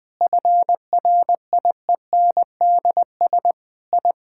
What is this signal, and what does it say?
Morse code, heard as a keyed tone, about 20 wpm, 700 Hz. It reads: FRIENDS I
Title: Sentences with 7-Word Phrases - 20wpm